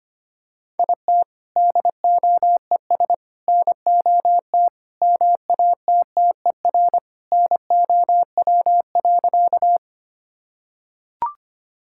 Morse code, heard as a keyed tone, about 25 wpm, 700 Hz. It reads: IT DOES NOT MATTER NOW.
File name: practice / 25wpm / Sentences with 4-Word Phrases - 25wpm